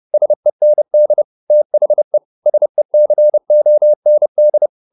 Morse code, heard as a keyed tone, about 30 wpm, 600 Hz. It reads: SEND THE SECOND